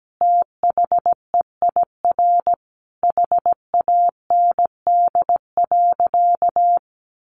Morse code, heard as a keyed tone, about 17 wpm, 700 Hz. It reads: THEIR HAND.
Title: Sentences from Top 1300 Words 17wpm